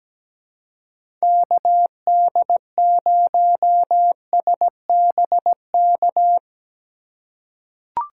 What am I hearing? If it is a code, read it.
Morse code, heard as a keyed tone, about 17 wpm, 700 Hz. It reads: KD0SBK
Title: USA Call Signs - 6 Characters - 2x Character Spacing - 17wpm